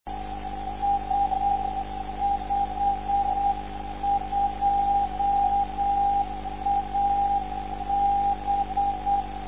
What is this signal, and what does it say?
Morse code, heard as a keyed tone, about 8 wpm, 800 Hz. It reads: S52AB
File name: lf zvoki / s52ab dl2hre